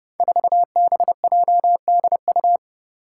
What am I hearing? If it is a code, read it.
Morse code, heard as a keyed tone, about 30 wpm, 700 Hz. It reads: 4BJDU
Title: Five Challenging Letter-Number Combinations 30wpm